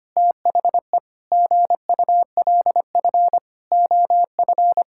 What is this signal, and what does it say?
Morse code, heard as a keyed tone, about 25 wpm, 700 Hz. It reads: THE GULF OF
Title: Pangram Challenge v2 - 25wpm